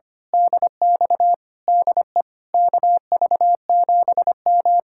Morse code, heard as a keyed tone, about 24 wpm, 700 Hz. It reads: DX DE KV7M
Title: Speed-Racing Common QSO Phrases - 25wpm